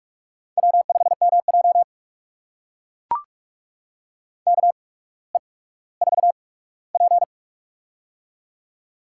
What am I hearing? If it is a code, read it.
Morse code, heard as a keyed tone, about 44 wpm, spaced as for 19 wpm, 700 Hz. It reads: W5MJ KE4P
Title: USA Call Signs - 4 Characters - 8x Character Spacing - 45wpm